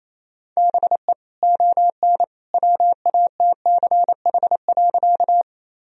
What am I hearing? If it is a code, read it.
Morse code, heard as a keyed tone, about 28 wpm, 700 Hz. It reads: BE ON WATCH.
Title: Sentences from Top 1000 Words - Review Part 1 - 28wpm